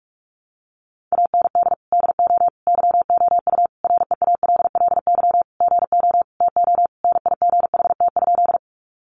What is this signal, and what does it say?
Morse code, heard as a keyed tone, about 44 wpm, 700 Hz. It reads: AND DO YOU REALLY GO TO NIGHT?
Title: Sentences from Top 600 Words 45wpm